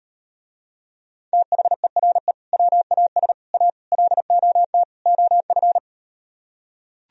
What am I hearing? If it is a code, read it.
Morse code, heard as a keyed tone, about 38 wpm, 700 Hz. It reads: THERE WAS A LOT OF